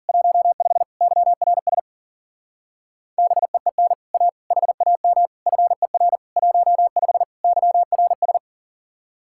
Morse code, heard as a keyed tone, about 50 wpm, 700 Hz. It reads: 15 YRS BEEN A HAM FER 15 YRS